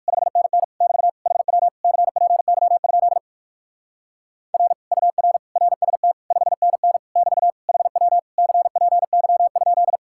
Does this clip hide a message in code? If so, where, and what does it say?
Yes: Morse code from 0.1 to 10.0 s, heard as a keyed tone, about 57 wpm, 700 Hz. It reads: 5NN <BT> HW CPY? R UR RST 5NN <BT> HW CPY?